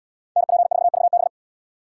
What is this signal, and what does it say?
Morse code, heard as a keyed tone, about 77 wpm, 700 Hz. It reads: NP4VB